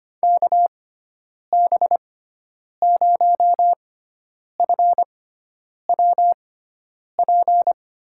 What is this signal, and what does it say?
Morse code, heard as a keyed tone, about 25 wpm, spaced as for 9 wpm, 700 Hz. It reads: KB0FWP
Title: USA Call Signs - 6 Characters - 6x Character Spacing - 25wpm